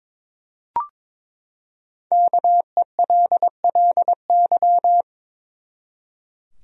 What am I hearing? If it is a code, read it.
Morse code, heard as a keyed tone, about 22 wpm, 700 Hz. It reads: KELLY
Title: USA Common Names - Max 5 Letters - 22wpm